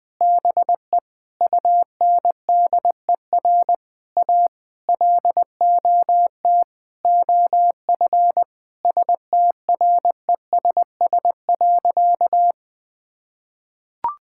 Morse code, heard as a keyed tone, about 20 wpm, 700 Hz. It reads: BE UNDER A LOT OF STRESS.
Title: Sentences from Top 1000 Words - Review Part 6 - 20wpm